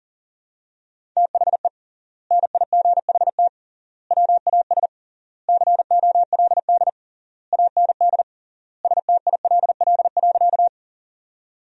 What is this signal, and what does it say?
Morse code, heard as a keyed tone, about 40 wpm, 700 Hz. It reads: THE NIGHT WAS COLD AND STILL.
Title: 3x Word-Spacing - Sentences from Top 400 Words - 40wpm